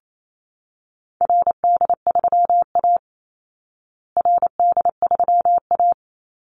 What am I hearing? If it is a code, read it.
Morse code, heard as a keyed tone, about 28 wpm, 700 Hz. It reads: RD3A RD3A
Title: Call Signs Repeated 2x - 28wpm